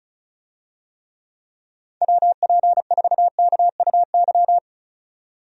Morse code, heard as a keyed tone, about 34 wpm, 700 Hz. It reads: WP4KUY
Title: USA Call Signs - 6 Characters - 35wpm